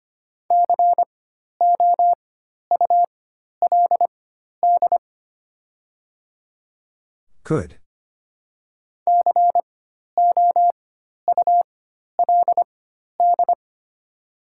Morse code, heard as a keyed tone, about 25 wpm, spaced as for 12 wpm, 690 Hz. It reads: COULD COULD
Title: 4x Character Spacing - Top 100 Words - 25wpm